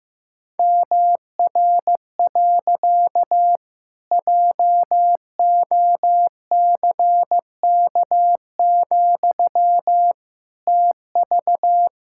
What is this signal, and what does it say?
Morse code, heard as a keyed tone, about 15 wpm, 700 Hz. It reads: MR. JOCK, TV
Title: Pangram Challenge v2 - 15wpm